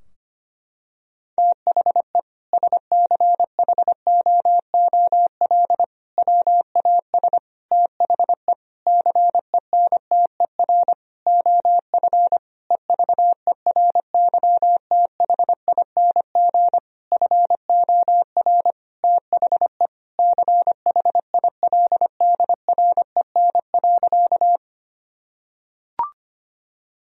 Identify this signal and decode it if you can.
Morse code, heard as a keyed tone, about 25 wpm, 700 Hz. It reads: THE SCHOOL WAS THE CENTER OF EVERYTHING FOR THE CHILDREN.
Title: Sentences from Top 1000 Words - Review Part 2 - 25wpm